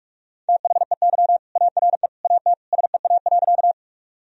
Morse code, heard as a keyed tone, about 46 wpm, 700 Hz. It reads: THEY ARE AT SEA.
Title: Sentences from Top 1000 Words - Review Part 1 - 45wpm